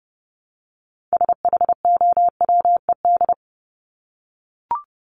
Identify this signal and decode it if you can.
Morse code, heard as a keyed tone, about 30 wpm, 700 Hz. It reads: SHOWED
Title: ED Suffix - Top 100 Words with Base Words 30wpm